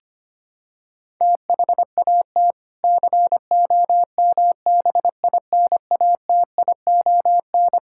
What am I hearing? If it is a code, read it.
Morse code, heard as a keyed tone, about 25 wpm, 700 Hz. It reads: THAT COMBINATION
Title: Sentences from Top 1000 Words - Review Part 6 - 25wpm